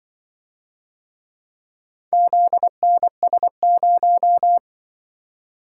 Morse code, heard as a keyed tone, about 24 wpm, 700 Hz. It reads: ZNS0
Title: Speed-Racing - Four Letter-Number Combinations - 20wpm